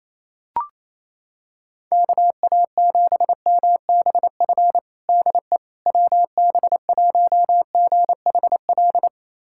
Morse code, heard as a keyed tone, about 28 wpm, 700 Hz. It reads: KA7MBF DE WB1GHL
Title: USA Callsign DE Callsign - 6 Characters - 28wpm